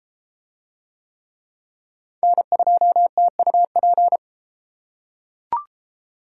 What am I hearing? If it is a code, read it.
Morse code, heard as a keyed tone, about 32 wpm, 700 Hz. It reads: N2TUP